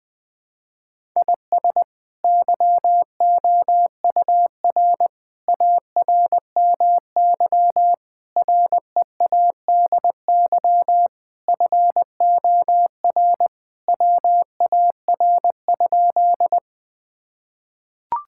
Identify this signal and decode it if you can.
Morse code, heard as a keyed tone, about 20 wpm, 700 Hz. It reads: IS YOUR ARMY READY FOR WAR?
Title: Sentences from Top 1000 Words - Review Part 5 - 20wpm